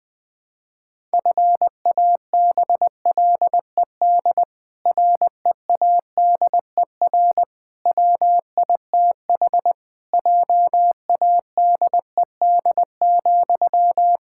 Morse code, heard as a keyed tone, about 20 wpm, 700 Hz. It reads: FABLED READER WITH JADED,
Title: Pangram Challenge v2 - 20wpm